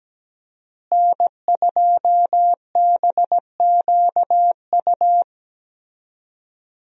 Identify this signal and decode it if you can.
Morse code, heard as a keyed tone, about 17 wpm, 700 Hz. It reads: N2BQU
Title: USA Call Signs - 5 Characters - Repeated 3x - 17wpm